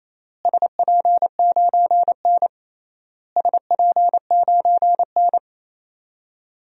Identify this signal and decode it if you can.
Morse code, heard as a keyed tone, about 28 wpm, 700 Hz. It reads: SP9N SP9N
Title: DXCC-100 Call Signs - 4 Characters - Repeated 3x - 28wpm